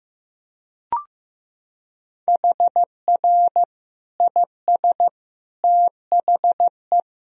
Morse code, heard as a keyed tone, about 15 wpm, 700 Hz. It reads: HR IS THE
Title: Sentences from Top 300 Words and Abbreviations 15wpm